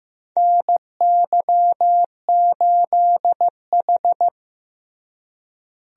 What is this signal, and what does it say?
Morse code, heard as a keyed tone, about 16 wpm, 700 Hz. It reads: NY8H
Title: USA Call Signs - 4 Characters - Repeated 3x - 15wpm